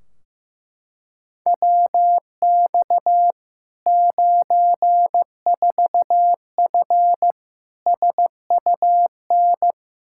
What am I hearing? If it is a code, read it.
Morse code, heard as a keyed tone, about 15 wpm, 700 Hz. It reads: WX 94F SUN
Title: Practice QSO - Exchange 2 - Info - 15wpm